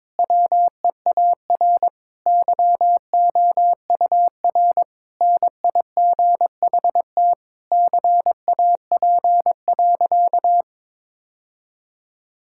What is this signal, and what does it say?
Morse code, heard as a keyed tone, about 22 wpm, 700 Hz. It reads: WEAR YOUR NIGHT CAP.
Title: Sentences from Top 1600 Words 22wpm